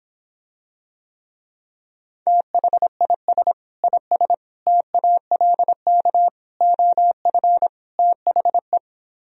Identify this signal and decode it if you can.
Morse code, heard as a keyed tone, about 26 wpm, 700 Hz. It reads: THIS IS TALK OF THE